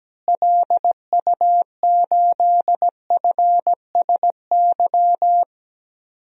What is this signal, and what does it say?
Morse code, heard as a keyed tone, about 17 wpm, 700 Hz. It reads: LU8FSY
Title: DXCC-100 Call Signs - 6 Characters - Repeated 2x - 17wpm